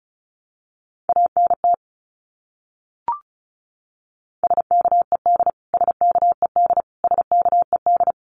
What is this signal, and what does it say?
Morse code, heard as a keyed tone, about 34 wpm, 700 Hz. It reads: ANT SKED SKED SKED
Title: Abbreviations Repeated 3x - 35wpm